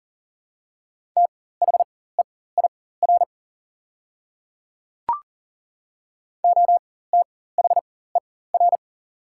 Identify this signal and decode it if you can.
Morse code, heard as a keyed tone, about 40 wpm, spaced as for 19 wpm, 700 Hz. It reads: THEIR OTHER
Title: 4x Character Spacing - Top 500 Words - 40wpm